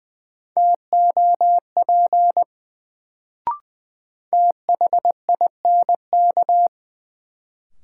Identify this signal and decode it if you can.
Morse code, heard as a keyed tone, about 20 wpm, 700 Hz. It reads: TOP THINK
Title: Sets of 2 Words - New Word Drill - 20wpm